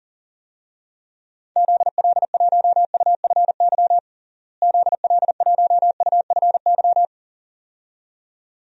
Morse code, heard as a keyed tone, about 40 wpm, 700 Hz. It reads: ZL1UFY ZL1UFY
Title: DXCC-100 Call Signs - 6 Characters - Repeated 2x - 40wpm